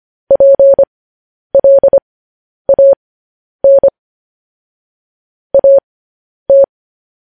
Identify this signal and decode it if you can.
Morse code, heard as a keyed tone, about 25 wpm, spaced as for 10 wpm, 550 Hz. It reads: PLAN AT
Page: news headlines in morse code at 10 wpm